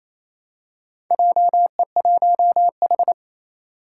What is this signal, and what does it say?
Morse code, heard as a keyed tone, about 28 wpm, 700 Hz. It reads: JE1H
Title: Four Challenging Letter-Number Combinations 28wpm